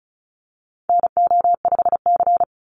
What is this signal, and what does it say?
Morse code, heard as a keyed tone, about 34 wpm, 700 Hz. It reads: NO5C
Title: USA Call Signs 35wpm